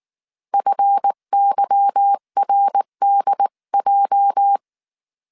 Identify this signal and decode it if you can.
Morse code, heard as a keyed tone, about 19 wpm, 800 Hz. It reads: FYRDJ
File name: CW / DET / 1 9 1500C S B 18wpm 2551